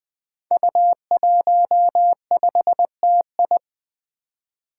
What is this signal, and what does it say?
Morse code, heard as a keyed tone, about 20 wpm, 700 Hz. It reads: U15TI
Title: Five Challenging Letter-Number Combinations 20wpm